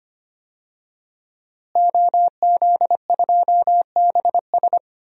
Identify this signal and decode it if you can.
Morse code, heard as a keyed tone, about 25 wpm, 700 Hz. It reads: OZ2BS
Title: DXCC-100 Call Signs - 5 Characters - Repeated 2x - 25wpm